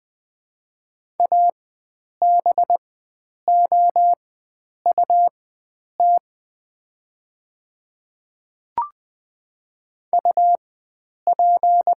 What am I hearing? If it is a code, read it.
Morse code, heard as a keyed tone, about 20 wpm, spaced as for 9 wpm, 700 Hz. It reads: ABOUT UP
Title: 4x Character Spacing - Top 100 Words - 20wpm